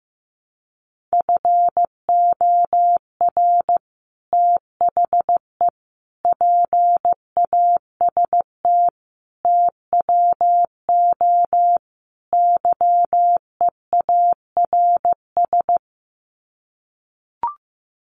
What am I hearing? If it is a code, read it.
Morse code, heard as a keyed tone, about 15 wpm, 700 Hz. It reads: FOR THE PAST TWO YEARS
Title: Sets of 5 Words Encore2 15wpm